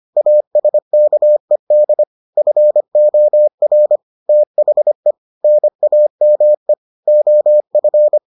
Morse code, heard as a keyed tone, about 25 wpm, 600 Hz. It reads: ASKED FOR THE NAME OF